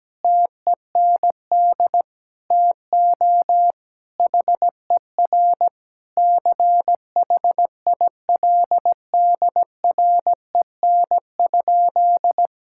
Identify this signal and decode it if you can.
Morse code, heard as a keyed tone, about 17 wpm, 700 Hz. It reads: TEND TO HER CHILDREN?
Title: Sentences from Top 1000 Words - Review Part 5 - 17wpm